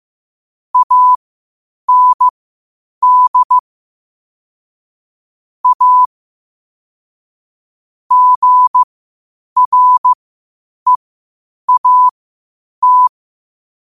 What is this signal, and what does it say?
Morse code, heard as a keyed tone, about 15 wpm, spaced as for 8 wpm, 1000 Hz. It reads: AND A GREAT
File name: qotdfiles / 8wpm